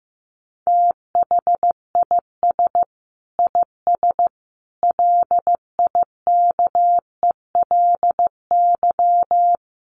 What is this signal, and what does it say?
Morse code, heard as a keyed tone, about 15 wpm, 700 Hz. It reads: THIS IS LIKELY
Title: Sentences from Top 1400 Words 15wpm